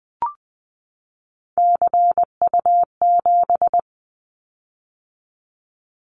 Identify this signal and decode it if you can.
Morse code, heard as a keyed tone, about 20 wpm, 700 Hz. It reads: CU7
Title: Callsign Prefixes for Top 100 DXCC - 20wpm